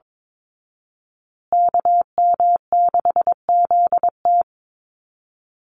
Morse code, heard as a keyed tone, about 22 wpm, 700 Hz. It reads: KM6ZT
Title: Call Signs Repeated 2x - 22wpm